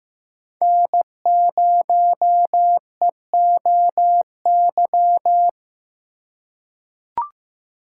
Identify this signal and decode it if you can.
Morse code, heard as a keyed tone, about 15 wpm, 700 Hz. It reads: N0EOY
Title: USA Call Signs - 5 Characters - 3x Character Spacing - 15wpm